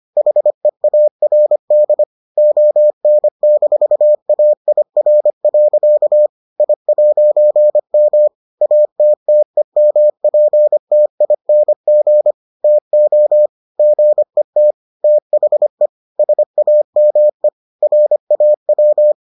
Morse code, heard as a keyed tone, about 25 wpm, 600 Hz. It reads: HEARD ON-AIR. I'M ATTEMPTING TO GET THE SAME RAW